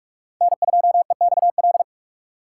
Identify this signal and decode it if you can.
Morse code, heard as a keyed tone, about 44 wpm, 700 Hz. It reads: N2EXL